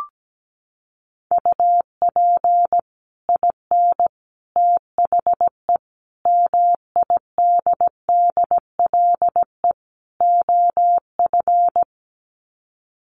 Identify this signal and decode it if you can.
Morse code, heard as a keyed tone, about 17 wpm, 700 Hz. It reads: UP IN THE MIDDLE OF
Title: Sets of 5 Words Encore 17wpm